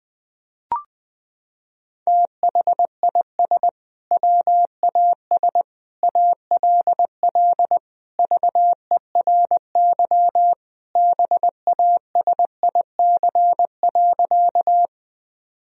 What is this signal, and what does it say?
Morse code, heard as a keyed tone, about 20 wpm, 700 Hz. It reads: THIS WAS ALL VERY BASIC.
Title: Sentences from Top 1000 Words - Review Part 3 - 20wpm